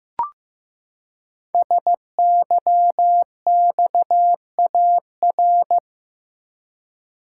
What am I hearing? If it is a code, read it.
Morse code, heard as a keyed tone, about 15 wpm, 700 Hz. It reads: SYXAR
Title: Code Groups - 5 Letters - 15wpm